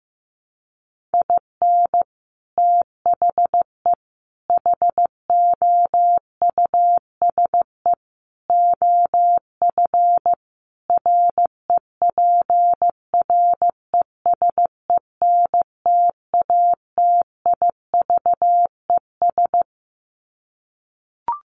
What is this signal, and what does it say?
Morse code, heard as a keyed tone, about 14 wpm, 700 Hz. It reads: IN THE HOUSE OF REPRESENTATIVES
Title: Sets of 5 Words Encore 15wpm